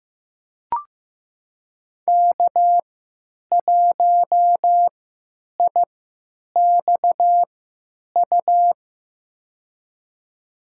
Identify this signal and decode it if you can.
Morse code, heard as a keyed tone, about 15 wpm, spaced as for 9 wpm, 700 Hz. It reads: K1IXU
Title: USA Call Signs - 5 Characters - 3x Character Spacing - 15wpm